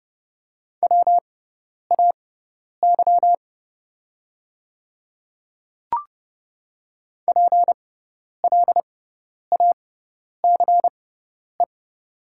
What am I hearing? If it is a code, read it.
Morse code, heard as a keyed tone, about 30 wpm, spaced as for 10 wpm, 690 Hz. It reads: WAY PLACE
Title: 6x Character Spacing - Top 500 Words - 30wpm